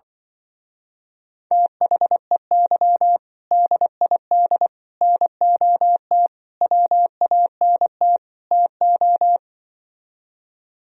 Morse code, heard as a keyed tone, about 24 wpm, 700 Hz. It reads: THEY DID NOT WANT TO
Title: Speed-Racing - Sets of 5 Words 20wpm